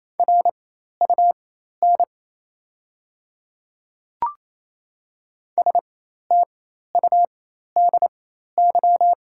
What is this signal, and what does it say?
Morse code, heard as a keyed tone, about 28 wpm, spaced as for 13 wpm, 700 Hz. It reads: RUN STUDY
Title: 4x Character Spacing - Top 300 Words - 28wpm